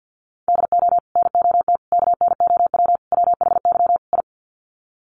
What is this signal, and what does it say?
Morse code, heard as a keyed tone, about 50 wpm, 700 Hz. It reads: DO NOT KNOW WHY I
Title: Sets of 5 Words Encore 50wpm